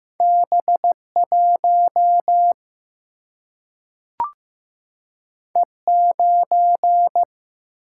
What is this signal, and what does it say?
Morse code, heard as a keyed tone, about 15 wpm, 700 Hz. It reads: B1 E9
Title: Single-Letter Single-Number 15wpm